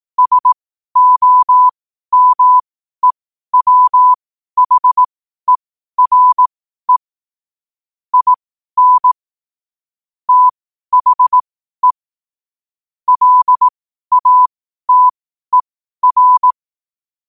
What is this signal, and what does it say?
Morse code, heard as a keyed tone, about 18 wpm, spaced as for 12 wpm, 1000 Hz. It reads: SOMEWHERE IN THE LATER